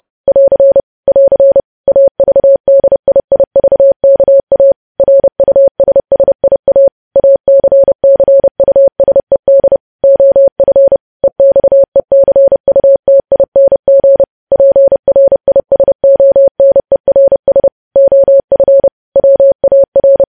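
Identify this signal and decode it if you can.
Morse code, heard as a keyed tone, about 30 wpm, 550 Hz. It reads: <AR> <AR> AVDIIVKA RUSSIA ACCUSED OF EXECUTING PRISONERS OF WAR
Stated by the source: Morse code news headlines practice, preceded by vvv.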